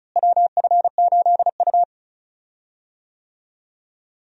Morse code, heard as a keyed tone, about 36 wpm, 700 Hz. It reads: WF8U
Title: USA Call Signs - 4 Characters - Repeated 2x - 35wpm